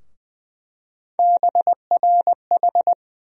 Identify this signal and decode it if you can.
Morse code, heard as a keyed tone, about 20 wpm, 700 Hz. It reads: BRH
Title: USA Call Sign Postfixes - 3 Letters 20wpm